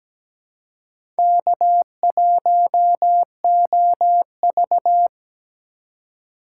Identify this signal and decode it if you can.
Morse code, heard as a keyed tone, about 17 wpm, 700 Hz. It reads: K1OV
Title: USA Call Signs - 4 Characters - 2x Character Spacing - 17wpm